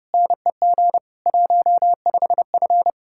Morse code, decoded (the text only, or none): NEG 15F